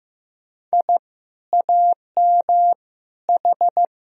Morse code, heard as a keyed tone, about 15 wpm, 700 Hz. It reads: I AM H
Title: Sentences from Top 1000 Words - Review Part 2 - 15wpm